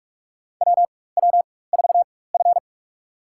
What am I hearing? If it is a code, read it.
Morse code, heard as a keyed tone, about 42 wpm, spaced as for 20 wpm, 700 Hz. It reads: WW4F
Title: USA Call Signs - 4x Character Spacing - 45wpm